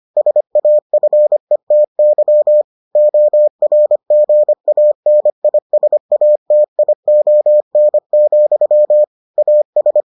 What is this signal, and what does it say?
Morse code, heard as a keyed tone, about 25 wpm, 600 Hz. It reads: SAFETY ORGANISATION, AS